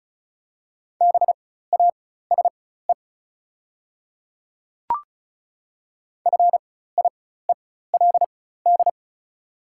Morse code, heard as a keyed tone, about 34 wpm, spaced as for 16 wpm, 690 Hz. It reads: BASE FIELD
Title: 4x Character Spacing - Top 500 Words - 35wpm